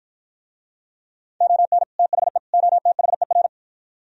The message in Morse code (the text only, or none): ON THE OTHER